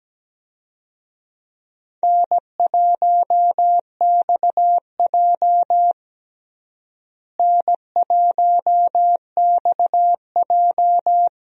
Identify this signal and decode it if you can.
Morse code, heard as a keyed tone, about 17 wpm, 700 Hz. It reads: N1XJ N1XJ
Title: USA Call Signs - 4 Characters - Repeated 2x - 17wpm